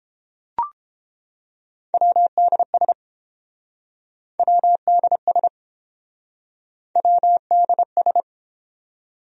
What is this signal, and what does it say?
Morse code, heard as a keyed tone, about 28 wpm, 700 Hz. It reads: WDS WDS WDS